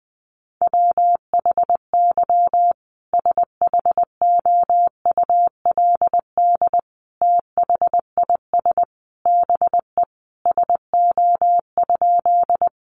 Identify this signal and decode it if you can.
Morse code, heard as a keyed tone, about 19 wpm, 700 Hz. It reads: WHY SHOULD THIS BE SO?
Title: Sentences from Top 300 Words 20wpm